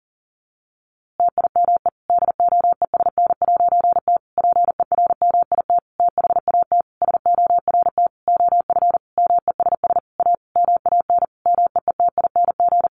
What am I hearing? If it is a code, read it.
Morse code, heard as a keyed tone, about 40 wpm, 700 Hz. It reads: TIME DOESN'T PERMIT THAT SORT OF MESS A MAN MEETING
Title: A Sound Of Thunder 40wpm